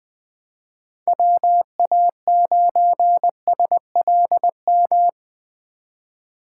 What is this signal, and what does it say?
Morse code, heard as a keyed tone, about 20 wpm, 700 Hz. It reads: WA9SLM
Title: USA Call Signs - 3x Character Spacing - 20wpm